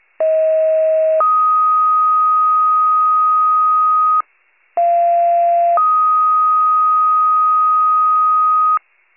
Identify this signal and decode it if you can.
Two-tone page, heard as a one-second tone, then a three-second tone, about 640 then 1250 Hz, about 690 then 1250 Hz. It is heard from 0.2 to 8.8 s.
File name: pager / 38 FIRE / wcalert